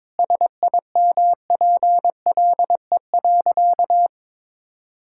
Morse code, heard as a keyed tone, about 22 wpm, 700 Hz. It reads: SIMPLE.
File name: practice / 22wpm / Sentences with 3-Word Phrases - 22wpm